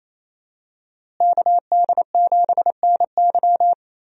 Morse code, heard as a keyed tone, about 28 wpm, 700 Hz. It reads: KD7NY
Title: USA Call Signs - 5 Characters - Repeated 3x - 28wpm